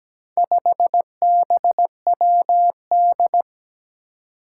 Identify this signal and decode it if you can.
Morse code, heard as a keyed tone, about 17 wpm, 700 Hz. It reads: 5BWD
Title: Four Challenging Letter-Number Combinations 17wpm